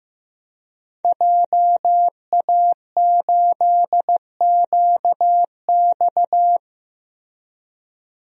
Morse code, heard as a keyed tone, about 15 wpm, 700 Hz. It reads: JA8QX
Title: DXCC-100 Call Signs - 5 Characters - Repeated 2x - 15wpm